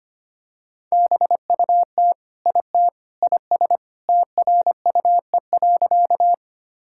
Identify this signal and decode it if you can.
Morse code, heard as a keyed tone, about 25 wpm, 700 Hz. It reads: BUT IT IS TRUE.
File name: practice / 25wpm / Sentences with 3-Word Phrases - 25wpm